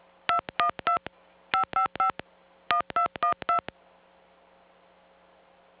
DTMF, heard as telephone touch tones, tones of about 110 ms. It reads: 212 555 1212